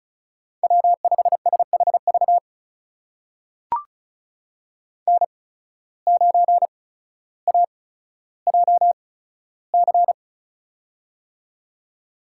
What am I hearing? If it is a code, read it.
Morse code, heard as a keyed tone, about 36 wpm, spaced as for 15 wpm, 700 Hz. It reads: W5SHV N9AJC
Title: USA Call Signs - 5 Characters - 8x Character Spacing - 35wpm